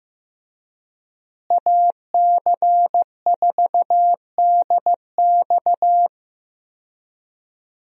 Morse code, heard as a keyed tone, about 15 wpm, 700 Hz. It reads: AC4DX
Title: USA Call Signs - 5 Characters - Repeated 3x - 15wpm